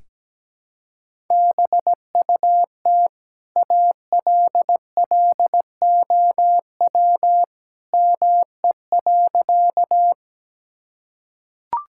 Morse code, heard as a keyed tone, about 17 wpm, 700 Hz. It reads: BUT ALLOW ME.
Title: Sentences from Top 1000 Words - Review Part 5 - 17wpm